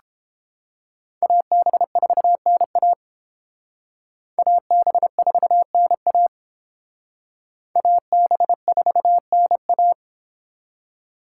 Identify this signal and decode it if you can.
Morse code, heard as a keyed tone, about 29 wpm, 700 Hz. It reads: AB4NA AB4NA AB4NA